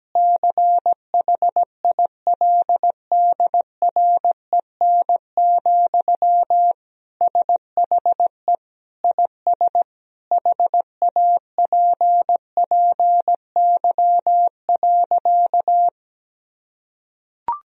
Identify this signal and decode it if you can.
Morse code, heard as a keyed tone, about 17 wpm, 700 Hz. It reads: CHILDREN, SHE IS HAPPY.
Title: Sentences from Top 1000 Words - Review Part 3 - 17wpm